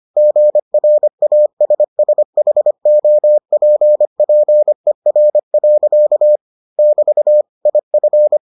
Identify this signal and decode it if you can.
Morse code, heard as a keyed tone, about 25 wpm, 600 Hz. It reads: GRASSHOPPER. <BT> IF